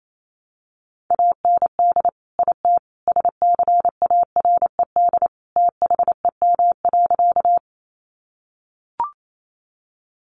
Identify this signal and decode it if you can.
Morse code, heard as a keyed tone, about 28 wpm, 700 Hz. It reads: AND IT SCARED THEM.
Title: Sentences from Top 700 Words 28wpm